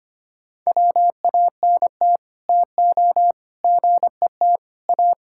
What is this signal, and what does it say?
Morse code, heard as a keyed tone, about 25 wpm, 700 Hz. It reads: WANT TO GET A
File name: Sets of 9 Words - 25wpm